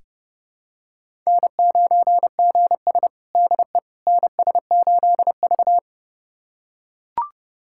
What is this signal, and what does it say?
Morse code, heard as a keyed tone, about 30 wpm, 700 Hz. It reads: N9GS DE NS8V
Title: USA Callsign DE Callsign - 4 Characters - 30wpm